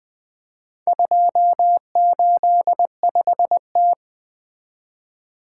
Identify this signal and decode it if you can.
Morse code, heard as a keyed tone, about 20 wpm, 700 Hz. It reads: 285T
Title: Four Letter-Number Combinations - 20wpm